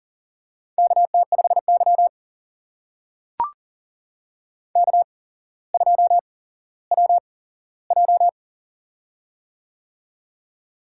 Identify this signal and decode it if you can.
Morse code, heard as a keyed tone, about 40 wpm, spaced as for 17 wpm, 700 Hz. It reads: KT5Y K2WJ
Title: USA Call Signs - 4 Characters - 8x Character Spacing - 40wpm